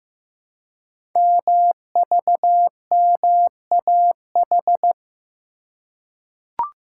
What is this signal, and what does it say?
Morse code, heard as a keyed tone, about 15 wpm, 700 Hz. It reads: MVMAH
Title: Code Groups - 5 Letters - 15wpm